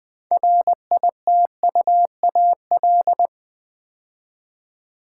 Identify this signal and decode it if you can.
Morse code, heard as a keyed tone, about 20 wpm, 700 Hz. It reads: RITUAL
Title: 6 Letter Words - First 3 Letters Given - 20wpm